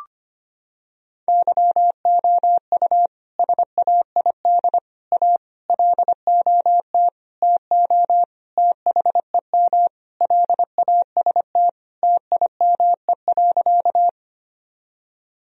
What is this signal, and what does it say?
Morse code, heard as a keyed tone, about 25 wpm, 700 Hz. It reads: YOU SAID A LOT TO THEM LAST TIME.
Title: Sentences from Top 1000 Words - Review Part 3 - 25wpm